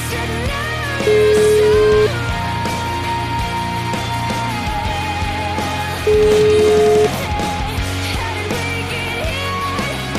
Ringback tone (Europe, 425 Hz), with noise.